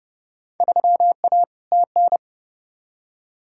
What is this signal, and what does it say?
Morse code, heard as a keyed tone, about 30 wpm, 700 Hz. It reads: 3A TN
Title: Field Day Exchanges - 30wpm